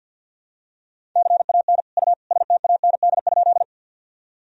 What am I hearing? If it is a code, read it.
Morse code, heard as a keyed tone, about 52 wpm, 700 Hz. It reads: CAN U STAND?
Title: Sentences from Top 300 Words and Abbreviations 50wpm